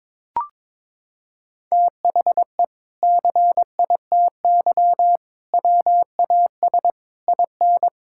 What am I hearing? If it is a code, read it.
Morse code, heard as a keyed tone, about 22 wpm, 700 Hz. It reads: THE CITY WAS IN
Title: Sentences from Top 1600 Words 22wpm